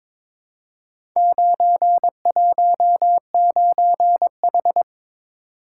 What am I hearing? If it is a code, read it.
Morse code, heard as a keyed tone, about 22 wpm, 700 Hz. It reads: 919H
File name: practice / 22wpm / Four Letter-Number Combinations - 22wpm